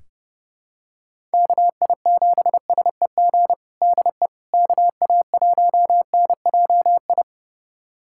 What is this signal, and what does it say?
Morse code, heard as a keyed tone, about 30 wpm, 700 Hz. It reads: KI7SEG DE KA1NJI
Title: USA Callsign DE Callsign - 6 Characters - 30wpm